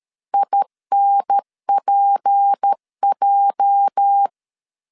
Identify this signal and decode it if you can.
Morse code, heard as a keyed tone, about 13 wpm, 800 Hz. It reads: INPJ
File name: CW / DET / 1 9 1500C S A 12wpm 3800